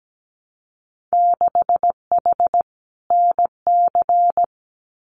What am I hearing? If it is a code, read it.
Morse code, heard as a keyed tone, about 17 wpm, 700 Hz. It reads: 6H NC
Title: Winter Field Day Exchanges Encore 17wpm